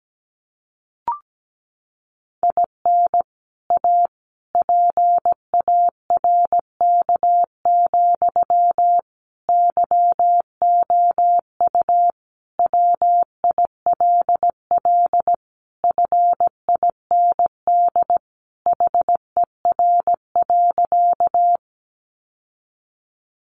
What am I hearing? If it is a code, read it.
Morse code, heard as a keyed tone, about 17 wpm, 700 Hz. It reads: IN A PARK, YOU WILL FIND HER.
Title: Sentences from Top 1300 Words 17wpm